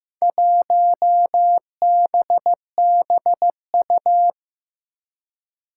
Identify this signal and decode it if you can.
Morse code, heard as a keyed tone, about 15 wpm, 700 Hz. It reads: 1BBU
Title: Four Challenging Letter-Number Combinations 15wpm